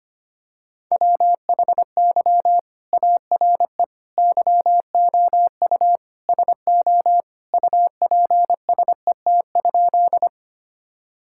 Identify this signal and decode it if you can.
Morse code, heard as a keyed tone, about 25 wpm, 700 Hz. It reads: WHY ARE YOU SO UPSET?
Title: Sentences from Top 1000 Words - Review Part 5 - 25wpm